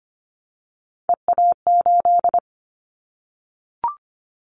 Morse code, heard as a keyed tone, about 25 wpm, 700 Hz. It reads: EA8
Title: Callsign Prefixes for Top 100 DXCC - 25wpm